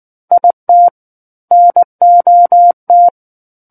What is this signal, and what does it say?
Morse code, heard as a keyed tone, about 19 wpm, 700 Hz. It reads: IT NOT
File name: Book One / 10-Chapter 10 - 19 WPM